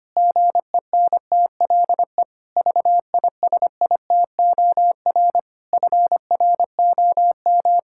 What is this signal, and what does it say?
Morse code, heard as a keyed tone, about 25 wpm, 700 Hz. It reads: GENTLE VISITOR FROM
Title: Pangram Challenge v2 - 25wpm